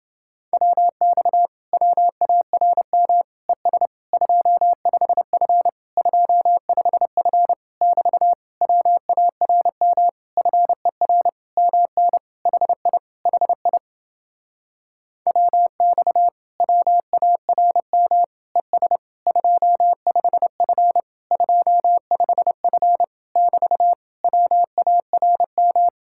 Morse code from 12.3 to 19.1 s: HI HI WX WARM ES

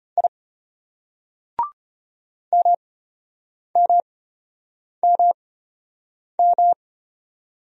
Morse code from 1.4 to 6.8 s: MMMM

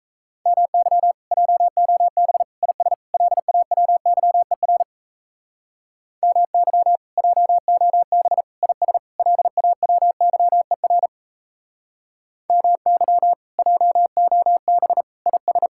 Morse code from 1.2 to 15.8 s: JOB IS LAWYER MY JOB IS LAWYER MY JOB IS

